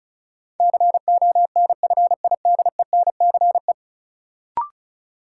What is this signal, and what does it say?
Morse code, heard as a keyed tone, about 36 wpm, 700 Hz. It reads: CONFIDENCE
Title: Sentences from Top 1500 Words - New Word Drill - 35wpm